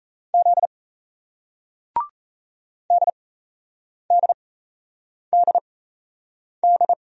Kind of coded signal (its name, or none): Morse code